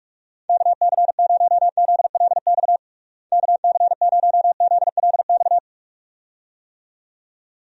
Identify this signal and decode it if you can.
Morse code, heard as a keyed tone, about 46 wpm, 700 Hz. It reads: KC0ZLX KC0ZLX
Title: USA Call Signs - 6 Characters - Repeated 2x - 45wpm